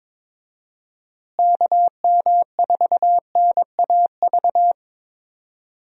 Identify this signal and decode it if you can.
Morse code, heard as a keyed tone, about 22 wpm, 700 Hz. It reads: KM4NAV
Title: USA Call Signs - 6 Characters - Repeated 3x - 22wpm